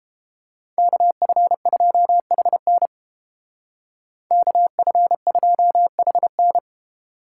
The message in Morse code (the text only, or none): KF2HN KF2HN